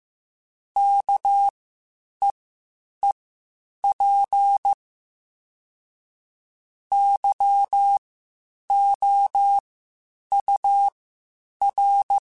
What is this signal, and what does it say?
Morse code, heard as a keyed tone, about 15 wpm, spaced as for 8 wpm, 800 Hz. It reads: KEEP YOUR
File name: Spaced / Chuckles Spaced-mp3s / SoftWords sp @13